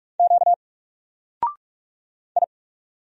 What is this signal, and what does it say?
Morse code, heard as a keyed tone, about 47 wpm, 700 Hz. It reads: Q I
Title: Single Letter-Number 45wpm